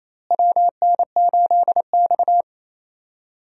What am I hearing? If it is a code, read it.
Morse code, heard as a keyed tone, about 28 wpm, 700 Hz. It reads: WN8X
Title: USA Call Signs - 4 Characters - Repeated 3x - 28wpm